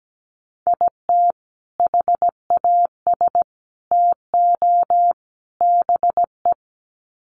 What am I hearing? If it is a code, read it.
Morse code, heard as a keyed tone, about 17 wpm, 700 Hz. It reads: IT HAS TO BE
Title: Sets of 4 Words 17wpm